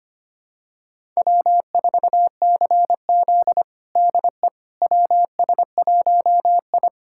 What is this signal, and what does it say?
Morse code, heard as a keyed tone, about 25 wpm, 700 Hz. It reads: W4CZ DE WS1I
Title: USA Callsign DE Callsign - 4 Characters - 25wpm